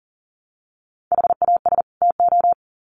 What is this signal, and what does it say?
Morse code, heard as a keyed tone, about 40 wpm, 700 Hz. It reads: HAS TO
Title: Sets of 2 Words 40wpm